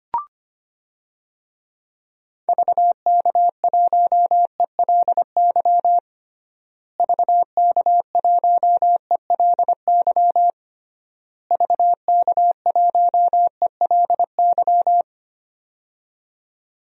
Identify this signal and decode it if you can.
Morse code, heard as a keyed tone, about 25 wpm, 700 Hz. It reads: VK1ELY VK1ELY VK1ELY
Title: DXCC-100 Call Signs - 6 Characters - Repeated 3x - 25wpm